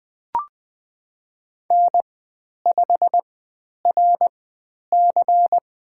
Morse code, heard as a keyed tone, about 20 wpm, spaced as for 10 wpm, 700 Hz. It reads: N5RC
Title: 102 Lesson 17g - Letter u - Call Signs 20wpm